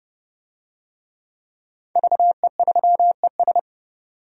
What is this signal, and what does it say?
Morse code, heard as a keyed tone, about 29 wpm, 700 Hz. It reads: VE3ES
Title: DXCC-100 Call Signs - 5 Characters - 30wpm